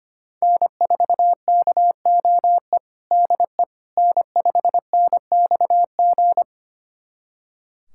Morse code, heard as a keyed tone, about 25 wpm, 700 Hz. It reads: N4KOE DE N5NXG
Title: USA Callsign DE Callsign - 5 Characters - 25wpm